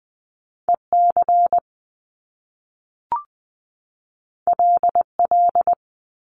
Morse code, heard as a keyed tone, about 20 wpm, 700 Hz. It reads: EC LL
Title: Top Two Letter Combinations in Words 20wpm